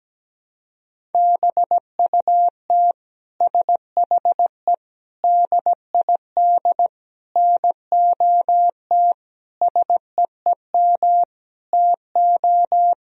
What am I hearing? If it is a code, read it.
Morse code, heard as a keyed tone, about 17 wpm, 700 Hz. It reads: BUT SHE DID NOT SEEM TO
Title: Sentences from Top 1000 Words - Review Part 1 - 17wpm